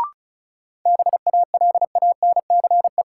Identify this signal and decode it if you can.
Morse code, heard as a keyed tone, about 36 wpm, 700 Hz. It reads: BALANCE
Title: Sentences from Top 900 Words - New Word Drill - 35wpm